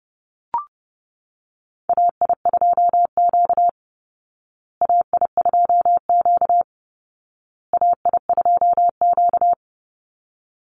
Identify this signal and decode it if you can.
Morse code, heard as a keyed tone, about 30 wpm, 700 Hz. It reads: AI2Q AI2Q AI2Q
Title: Call Signs Repeated 2x - 30wpm